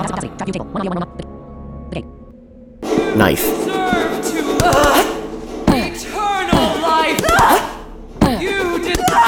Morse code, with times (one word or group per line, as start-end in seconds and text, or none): none